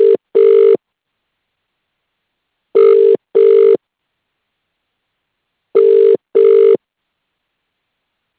Ringback tone (UK), 400 + 450 Hz.